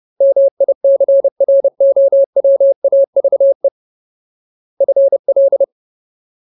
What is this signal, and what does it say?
Morse code, heard as a keyed tone, about 30 wpm, 550 Hz. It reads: MICROWAVE FL
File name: Ultimate Word List / 30WPM Ultimate